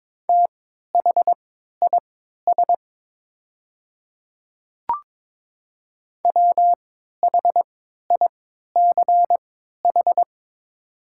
Morse code, heard as a keyed tone, about 22 wpm, spaced as for 12 wpm, 690 Hz. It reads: THIS WHICH
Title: 3x Character Spacing - Top 300 Words - 22wpm